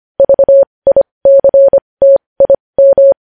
Morse code, heard as a keyed tone, about 25 wpm, spaced as for 20 wpm, 550 Hz. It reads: VICTIM
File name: podcast / News headlines-1x-25x20